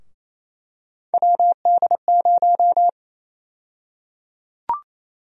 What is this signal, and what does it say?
Morse code, heard as a keyed tone, about 28 wpm, 700 Hz. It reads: WD0
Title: USA Call Sign Prefixes - Repeated 2x - 28wpm